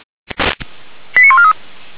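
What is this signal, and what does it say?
Five-tone selective call (ZVEI), heard as quick runs of five tones, tones of about 70 ms. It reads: 89143
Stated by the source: ZVEI five-tone selective call received on an SDR.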